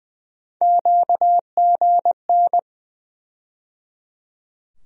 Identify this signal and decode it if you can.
Morse code, heard as a keyed tone, about 20 wpm, 700 Hz. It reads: QGN
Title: USA Call Sign Postfixes - 3 Letters 20wpm